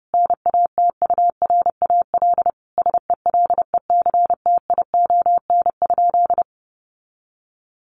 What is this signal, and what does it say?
Morse code, heard as a keyed tone, about 30 wpm, 700 Hz. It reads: NATURAL SELECTION?
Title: Sentences from Top 1400 Words 30wpm